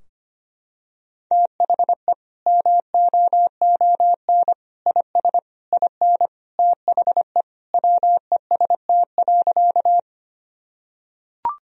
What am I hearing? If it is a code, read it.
Morse code, heard as a keyed tone, about 25 wpm, 700 Hz. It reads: THE MOON IS IN THE WEST.
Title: Sentences from Top 1000 Words - Review Part 3 - 25wpm